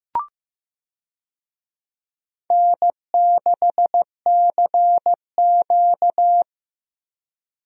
Morse code, heard as a keyed tone, about 15 wpm, 700 Hz. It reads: N6CQ
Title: USA Call Signs - 4 Characters - Repeated 2x - 15wpm